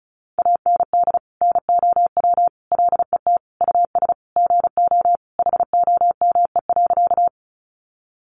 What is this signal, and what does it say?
Morse code, heard as a keyed tone, about 34 wpm, 700 Hz. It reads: AND NOW LET US GO HOME.
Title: Sentences from Top 300 Words 35wpm